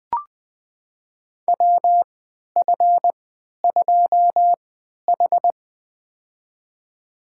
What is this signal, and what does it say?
Morse code, heard as a keyed tone, about 20 wpm, spaced as for 11 wpm, 700 Hz. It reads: WF2H
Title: USA Call Signs - 4 Characters - 3x Character Spacing - 20wpm